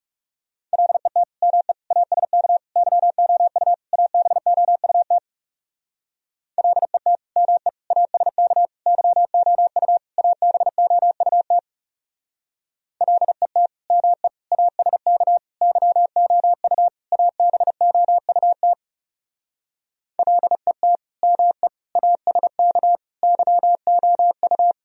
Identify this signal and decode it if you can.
Morse code, heard as a keyed tone, about 36 wpm, 700 Hz. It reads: LET ME ASK YOU ABOUT LET ME ASK YOU ABOUT LET ME ASK YOU ABOUT LET ME ASK YOU
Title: Speed-Racing - Sets of 5 Words 30wpm